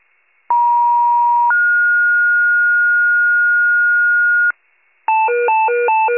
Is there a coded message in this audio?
Yes: two-tone page from 0.5 to 4.5 s, heard as a one-second tone, then a three-second tone, about 950 then 1430 Hz, with noise.